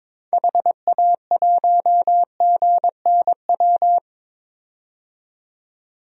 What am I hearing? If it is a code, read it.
Morse code, heard as a keyed tone, about 22 wpm, 700 Hz. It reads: HA1GNW
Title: DXCC-100 Call Signs - 6 Characters - 22wpm